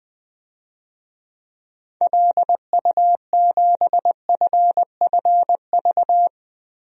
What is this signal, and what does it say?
Morse code, heard as a keyed tone, about 20 wpm, 700 Hz. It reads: LU7FFV
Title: DXCC-100 Call Signs - 6 Characters - 20wpm